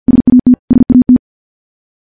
Morse code, heard as a keyed tone, about 50 wpm, 250 Hz. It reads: SAT SAT
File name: morse / morse-training-50WPM